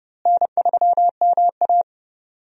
Morse code, heard as a keyed tone, about 32 wpm, 700 Hz. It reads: N3MA
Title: USA Call Signs - 4 Characters - Repeated 3x - 30wpm